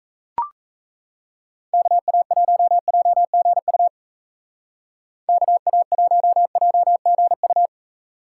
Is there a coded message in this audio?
Yes: Morse code from 1.7 to 7.7 s, heard as a keyed tone, about 39 wpm, 700 Hz. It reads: KA1JGU KA1JGU